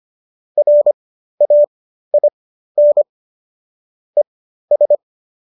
Morse code, heard as a keyed tone, about 25 wpm, spaced as for 13 wpm, 590 Hz. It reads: RAIN ES